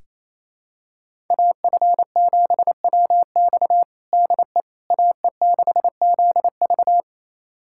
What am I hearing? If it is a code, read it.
Morse code, heard as a keyed tone, about 28 wpm, 700 Hz. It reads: AF7WX DE AE6ZV
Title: USA Callsign DE Callsign - 5 Characters - 28wpm